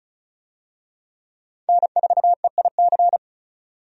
Morse code, heard as a keyed tone, about 36 wpm, 700 Hz. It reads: N4EIC
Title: USA Call Signs - 5 Characters - 35wpm